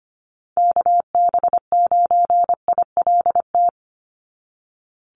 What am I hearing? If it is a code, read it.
Morse code, heard as a keyed tone, about 25 wpm, 700 Hz. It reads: KB9ILT
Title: NA SOTA Chaser Callsigns 25wpm